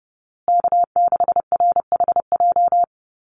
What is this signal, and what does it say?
Morse code, heard as a keyed tone, about 30 wpm, 700 Hz. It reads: K6RHJ
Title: USA Call Signs Repeated 3x 30wpm